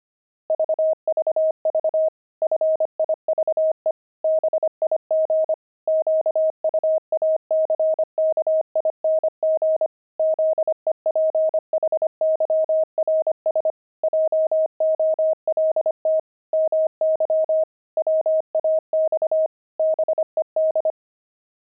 Morse code, heard as a keyed tone, about 25 wpm, 630 Hz. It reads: VVV FIVE BIG QUACKING ZEPHYRS JOLT MY WAX BED
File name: pangram-fs-25-cs-25-five